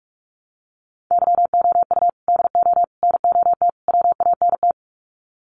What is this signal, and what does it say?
Morse code, heard as a keyed tone, about 44 wpm, 700 Hz. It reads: YOU DO NOT WANT
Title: Sets of 4 Words 45wpm